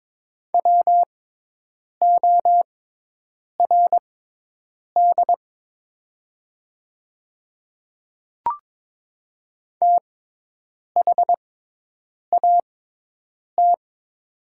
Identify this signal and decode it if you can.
Morse code, heard as a keyed tone, about 22 wpm, spaced as for 8 wpm, 700 Hz. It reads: WORD THAT
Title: 6x Character Spacing - Top 100 Words - 22wpm